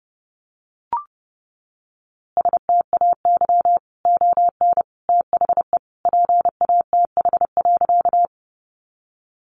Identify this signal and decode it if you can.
Morse code, heard as a keyed tone, about 28 wpm, 700 Hz. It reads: STAY ON THE PATH.
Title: A Sound Of Thunder 30wpm